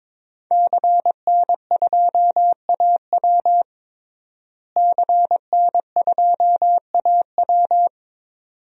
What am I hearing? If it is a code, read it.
Morse code, heard as a keyed tone, about 22 wpm, 700 Hz. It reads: CN2AW CN2AW
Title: DXCC-100 Call Signs - 5 Characters - Repeated 2x - 22wpm